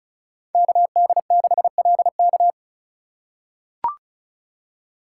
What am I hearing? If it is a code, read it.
Morse code, heard as a keyed tone, about 36 wpm, 700 Hz. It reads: KD6LK
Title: USA Call Signs - 2x Character Spacing - 35wpm